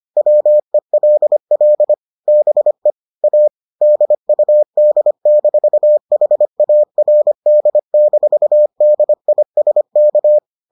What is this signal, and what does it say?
Morse code, heard as a keyed tone, about 25 wpm, 600 Hz. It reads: WELL BE A DUD-HARD-DISK